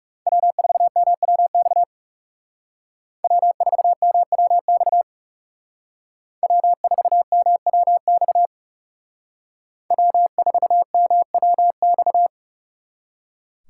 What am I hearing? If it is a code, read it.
Morse code, heard as a keyed tone, about 38 wpm, 700 Hz. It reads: W4MWX W4MWX W4MWX W4MWX